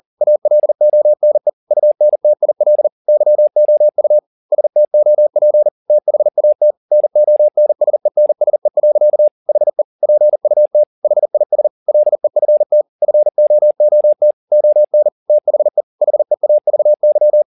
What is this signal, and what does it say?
Morse code, heard as a keyed tone, about 40 wpm, 600 Hz. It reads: ALONE UNTIL YOU STOP THAT NONSENSE. HE PUT HIS LEFT FOOT ON THE HEAVY